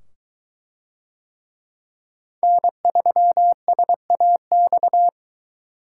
Morse code, heard as a keyed tone, about 23 wpm, 700 Hz. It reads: N3SAX